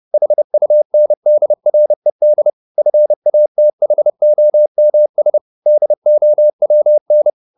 Morse code, heard as a keyed tone, about 30 wpm, 600 Hz. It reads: HUNDRED FATHOMS DOWN
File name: the old man and the sea / the old man and the sea 1 30wpm